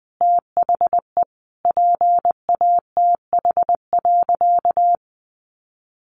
Morse code, heard as a keyed tone, about 19 wpm, 700 Hz. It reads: THE PATH.
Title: A Sound Of Thunder 20wpm